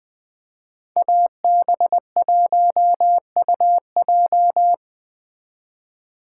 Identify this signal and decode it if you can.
Morse code, heard as a keyed tone, about 20 wpm, 700 Hz. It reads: AB1UJ